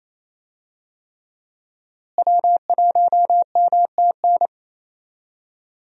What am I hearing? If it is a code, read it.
Morse code, heard as a keyed tone, about 27 wpm, 700 Hz. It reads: W1MTN
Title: USA Call Signs - 5 Characters - Repeated 3x - 28wpm